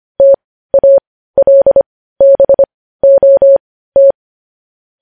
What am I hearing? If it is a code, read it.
Morse code, heard as a keyed tone, about 25 wpm, spaced as for 15 wpm, 550 Hz. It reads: TALBOT
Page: news-headlines-in-morse-code-at-15-wpm